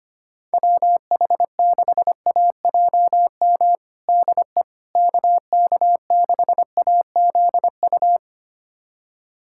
Morse code, heard as a keyed tone, about 25 wpm, 700 Hz. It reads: WH6AJM DE KK6AZU
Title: USA Callsign DE Callsign 25wpm